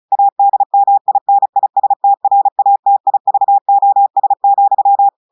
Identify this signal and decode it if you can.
Morse code, heard as a keyed tone, about 35 wpm, 800 Hz. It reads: ADMINISTRATIVOS,
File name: portugais texte en clair 2 vitesse 35 mots minutes